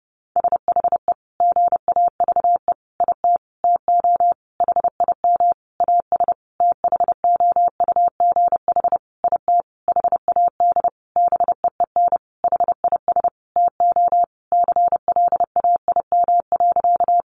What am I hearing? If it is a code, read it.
Morse code, heard as a keyed tone, about 30 wpm, 700 Hz. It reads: SHE GAVE IT TO HIM AS THOUGH IT HAD BEEN HIS TO CLAIM.
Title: Sentences from Top 1500 Words 30wpm